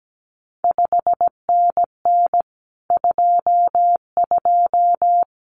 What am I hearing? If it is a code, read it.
Morse code, heard as a keyed tone, about 17 wpm, 700 Hz. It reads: 5NN 22
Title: CQ WW DX Contest Exchange Encore - 17wpm